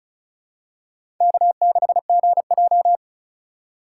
Morse code, heard as a keyed tone, about 34 wpm, 700 Hz. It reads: K6GJ
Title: USA Call Signs - 2x Character Spacing - 35wpm